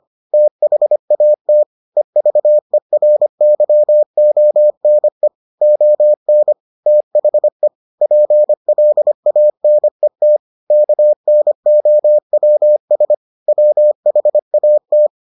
Morse code, heard as a keyed tone, about 25 wpm, 600 Hz. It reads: THAT EVERYONE ON THE PLANET KNOWS WHAT